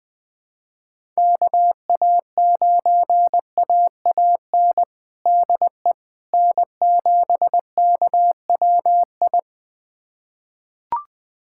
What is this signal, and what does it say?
Morse code, heard as a keyed tone, about 20 wpm, 700 Hz. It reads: KA9AAN DE N7KWI
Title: USA Callsign DE Callsign 20wpm